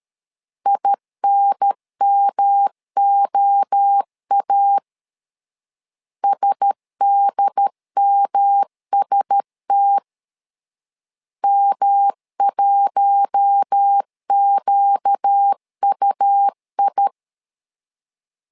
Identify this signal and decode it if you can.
Morse code, heard as a keyed tone, about 13 wpm, 800 Hz. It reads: INMOA SDMST M1QUI
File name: LEZIONI / 4 Q1S 12wpm 1802